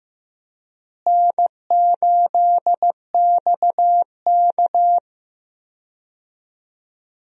Morse code, heard as a keyed tone, about 15 wpm, 700 Hz. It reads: N8XK
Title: USA Call Signs - 4 Characters - Repeated 3x - 15wpm